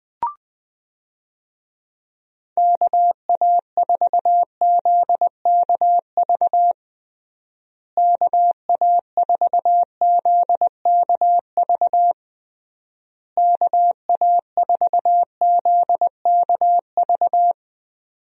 Morse code, heard as a keyed tone, about 20 wpm, 700 Hz. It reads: KA4ZKV KA4ZKV KA4ZKV
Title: USA Call Signs - 6 Characters - Repeated 3x - 20wpm